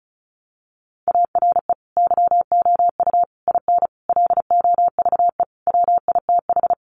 Morse code, heard as a keyed tone, about 34 wpm, 700 Hz. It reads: ARE YOU IN LOVE WITH
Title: Sentences from Top 400 Words 35wpm